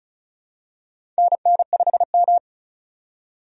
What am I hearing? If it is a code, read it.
Morse code, heard as a keyed tone, about 36 wpm, 700 Hz. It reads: NN5M
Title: USA Call Signs - 4 Characters - 35wpm